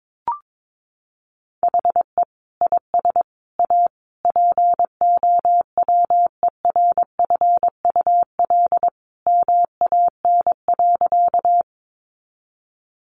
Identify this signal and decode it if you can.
Morse code, heard as a keyed tone, about 22 wpm, 700 Hz. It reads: HE IS A POWERFUL MAN.
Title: Sentences from Top 1300 Words 22wpm